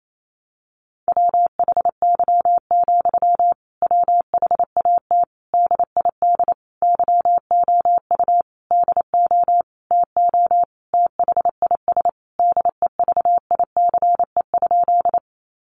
Morse code, heard as a keyed tone, about 28 wpm, 700 Hz. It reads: WHY, WHAT DID YOU DO TO THIS DEVICE?
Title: Sentences from Top 1500 Words 28wpm